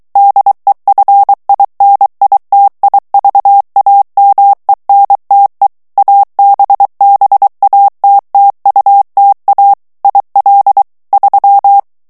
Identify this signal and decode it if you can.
Morse code, heard as a keyed tone, about 23 wpm, 800 Hz. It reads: DEFINITIVAMENTE ABBATTUTA IL 3
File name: ITALIEN texte clair 1 vitesse 22 mots minute